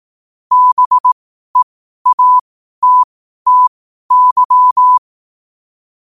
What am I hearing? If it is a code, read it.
Morse code, heard as a keyed tone, about 18 wpm, spaced as for 12 wpm, 1000 Hz. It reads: BEATTY
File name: qotdfiles / 12wpm